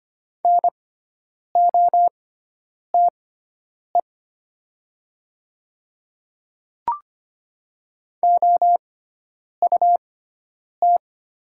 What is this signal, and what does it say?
Morse code, heard as a keyed tone, about 25 wpm, spaced as for 9 wpm, 700 Hz. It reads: NOTE OUT
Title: 6x Character Spacing - Top 500 Words - 25wpm